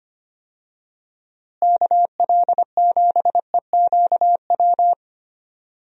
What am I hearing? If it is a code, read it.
Morse code, heard as a keyed tone, about 25 wpm, 700 Hz. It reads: KL7EQW
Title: DXCC-100 Call Signs - 6 Characters - Repeated 3x - 25wpm